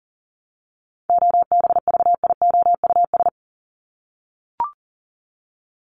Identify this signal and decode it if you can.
Morse code, heard as a keyed tone, about 40 wpm, 700 Hz. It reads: OBVIOUS
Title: LY Suffix - Top 100 Words with Base Words 40wpm